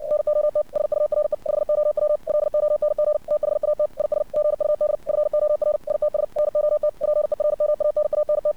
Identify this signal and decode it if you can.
Morse code, heard as a keyed tone, about 64 wpm, 600 Hz. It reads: NOT SURE HOW LONG THAT IS GUD FOR ITS NOT PERMANANT
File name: archived / CRSnet